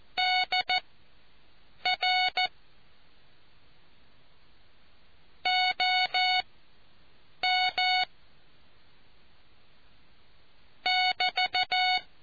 Morse code, heard as a keyed tone, about 14 wpm, spaced as for 6 wpm, 740 Hz. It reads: DR OM <BT>